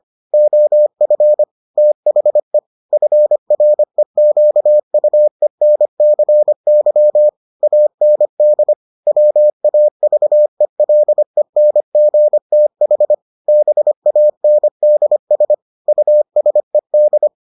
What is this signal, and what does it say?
Morse code, heard as a keyed tone, about 25 wpm, 600 Hz. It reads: OF THE FREQUENCY AND WAVELENGTH BANDS USED